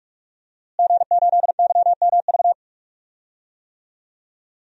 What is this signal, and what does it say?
Morse code, heard as a keyed tone, about 43 wpm, 700 Hz. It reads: G8YMV
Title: DXCC-100 Call Signs - 5 Characters - Repeated 2x - 45wpm